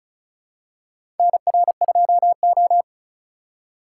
Morse code, heard as a keyed tone, about 34 wpm, 700 Hz. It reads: NR2O
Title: USA Call Signs - 4 Characters - 4x Character Spacing - 35wpm